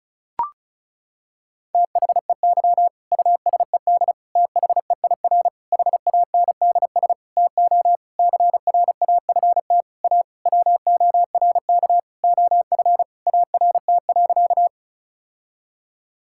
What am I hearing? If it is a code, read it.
Morse code, heard as a keyed tone, about 34 wpm, 700 Hz. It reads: THEY USED THEIR HANDS TO CRAFT A WORK OF ART.
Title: Sentences from Top 1000 Words - Review Part 6 - 35wpm